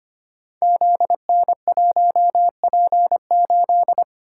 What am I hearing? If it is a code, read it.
Morse code, heard as a keyed tone, about 25 wpm, 700 Hz. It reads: ZN1P8
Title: Code Groups - 5 Letter:Numbers - 25wpm